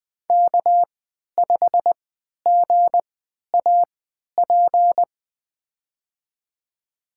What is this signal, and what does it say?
Morse code, heard as a keyed tone, about 20 wpm, spaced as for 11 wpm, 700 Hz. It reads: K5GAP
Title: USA Call Signs - 5 Characters - 3x Character Spacing - 20wpm